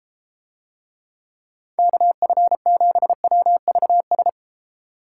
Morse code, heard as a keyed tone, about 32 wpm, 700 Hz. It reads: KF7WVS